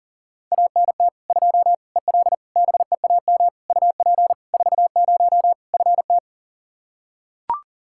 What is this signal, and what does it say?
Morse code, heard as a keyed tone, about 40 wpm, 700 Hz. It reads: ANT 2 EL BEAM UP 40 FT
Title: K1USN SST QSO Practice 40wpm